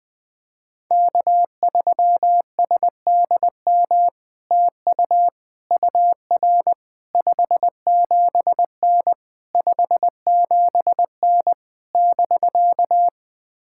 Morse code, heard as a keyed tone, about 20 wpm, 700 Hz. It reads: K3SDM TU UR 57N 57N <BK>
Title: POTA QSO Practice - 20wpm